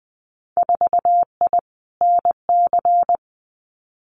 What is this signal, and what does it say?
Morse code, heard as a keyed tone, about 20 wpm, 700 Hz. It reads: 4I NC
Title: Winter Field Day Exchanges 20wpm